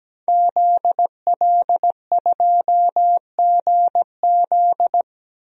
Morse code, heard as a keyed tone, about 17 wpm, 700 Hz. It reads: ZL2GZ
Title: DXCC-100 Call Signs - 5 Characters - Repeated 3x - 17wpm